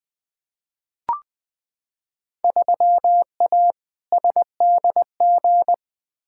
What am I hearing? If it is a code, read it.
Morse code, heard as a keyed tone, about 20 wpm, 700 Hz. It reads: 3A SDG
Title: Field Day Exchanges - 20wpm